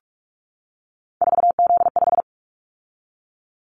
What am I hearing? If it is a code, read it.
Morse code, heard as a keyed tone, about 46 wpm, 700 Hz. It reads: 4Z5
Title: Callsign Prefixes for Top 50 DXCC - 45wpm